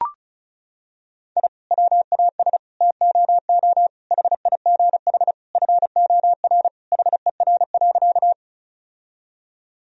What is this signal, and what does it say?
Morse code, heard as a keyed tone, about 34 wpm, 700 Hz. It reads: I WAS TOO HIGH FOR HER.
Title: Sentences from Top 1000 Words - Review Part 1 - 35wpm